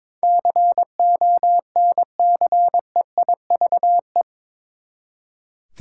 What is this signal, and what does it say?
Morse code, heard as a keyed tone, about 22 wpm, 700 Hz. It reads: CONCEIVE
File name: practice / 22wpm / Eight Letter Words - 22wpm